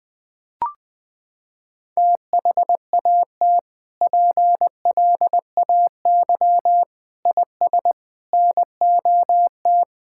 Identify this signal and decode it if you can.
Morse code, heard as a keyed tone, about 20 wpm, 700 Hz. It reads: THAT PLAY IS NOT
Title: Sentences from Top 1000 Words - Review Part 5 - 20wpm